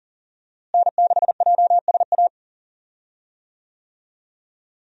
Morse code, heard as a keyed tone, about 40 wpm, 700 Hz. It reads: N6JSA
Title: USA Call Signs - 5 Characters - Repeated 3x - 40wpm